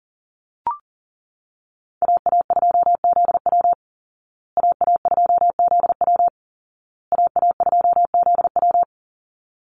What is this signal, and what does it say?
Morse code, heard as a keyed tone, about 40 wpm, 700 Hz. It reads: AA2ZW AA2ZW AA2ZW
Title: Call Signs Repeated 2x - 40wpm